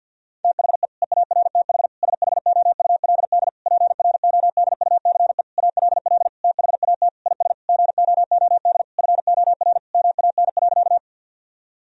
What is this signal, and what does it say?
Morse code, heard as a keyed tone, about 52 wpm, 700 Hz. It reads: THE EARTH SHOULD PRODUCE ALL THAT IS GOOD FOR MAN.
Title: Sentences from Top 1000 Words - Review Part 3 - 50wpm